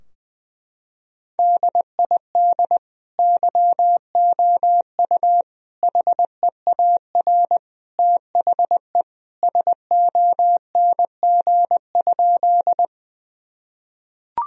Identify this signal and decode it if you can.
Morse code, heard as a keyed tone, about 21 wpm, 700 Hz. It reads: DID YOU HEAR THE SONG?
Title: Sentences from Top 1000 Words - Review Part 2 - 20wpm